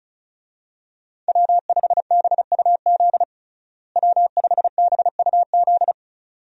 Morse code, heard as a keyed tone, about 36 wpm, 700 Hz. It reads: W5BUZ W5BUZ
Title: USA Call Signs - 5 Characters - Repeated 2x - 35wpm